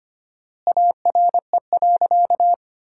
Morse code, heard as a keyed tone, about 25 wpm, 700 Hz. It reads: ARE.
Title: 4x Word-Spacing - Sentences from Top 500 Words - 25wpm